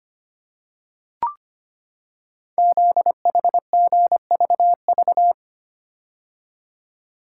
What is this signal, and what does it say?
Morse code, heard as a keyed tone, about 25 wpm, 700 Hz. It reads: ZHGVV
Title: Code Groups - 5 Letters - 25wpm